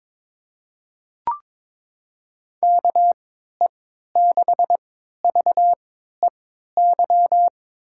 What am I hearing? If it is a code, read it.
Morse code, heard as a keyed tone, about 21 wpm, spaced as for 12 wpm, 700 Hz. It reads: KE6VEY
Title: USA Call Signs - 6 Characters - 3x Character Spacing - 22wpm